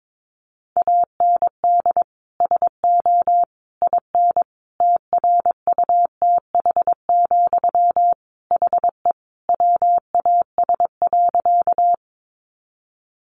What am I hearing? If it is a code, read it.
Morse code, heard as a keyed tone, about 22 wpm, 700 Hz. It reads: AND SO IN TRUTH, HE WAS.
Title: Sentences from Top 800 Words 22wpm